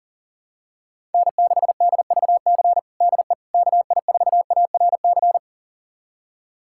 Morse code, heard as a keyed tone, about 40 wpm, 700 Hz. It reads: N6DVC DE KI4ARC
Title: USA Callsign DE Callsign 40wpm